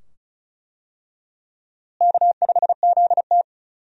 Morse code, heard as a keyed tone, about 36 wpm, 700 Hz. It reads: K5ZT
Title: USA Call Signs - 4 Characters - Repeated 2x - 35wpm